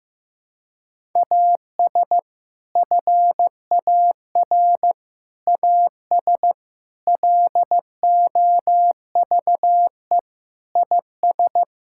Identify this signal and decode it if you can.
Morse code, heard as a keyed tone, about 15 wpm, 700 Hz. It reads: AS FAR AS LOVE IS
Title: Sentences from Top 1000 Words - Review Part 6 - 15wpm